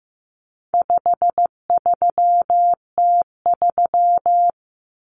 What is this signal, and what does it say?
Morse code, heard as a keyed tone, about 15 wpm, 700 Hz. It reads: 53T3
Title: Common Cut Numbers - 4-Digit 15wpm